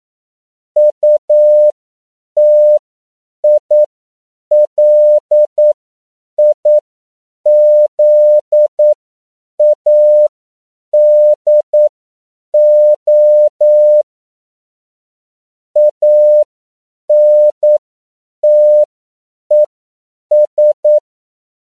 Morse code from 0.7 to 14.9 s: UTILIZADO